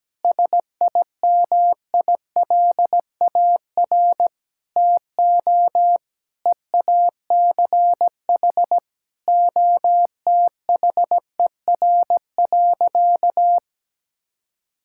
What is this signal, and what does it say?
Morse code, heard as a keyed tone, about 17 wpm, 700 Hz. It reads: SIMILAR TO EACH OTHER.
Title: Sentences from Top 1000 Words - Review Part 5 - 17wpm